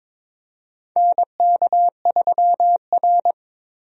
Morse code, heard as a keyed tone, about 21 wpm, 700 Hz. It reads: NK3R
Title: USA Call Signs - 4 Characters - Repeated 3x - 22wpm